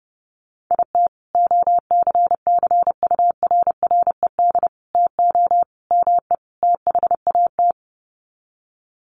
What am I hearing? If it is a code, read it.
Morse code, heard as a keyed tone, about 30 wpm, 700 Hz. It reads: IT OCCURRED TO ME THAT
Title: Sets of 5 Words Encore 30wpm